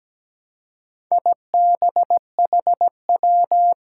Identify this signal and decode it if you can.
Morse code, heard as a keyed tone, about 17 wpm, 700 Hz. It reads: IBHW
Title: Four Challenging Letter-Number Combinations 17wpm